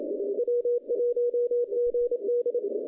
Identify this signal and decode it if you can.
Morse code, heard as a keyed tone, about 30 wpm, 470 Hz. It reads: W1GD
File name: 2015 cq ww cw / 2015 cqwww k5tr - 0-47